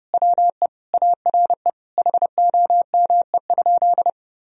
Morse code, heard as a keyed tone, about 32 wpm, 700 Hz. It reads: WE ARE HOME?
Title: Sentences from Top 1000 Words - Review Part 1 - 30wpm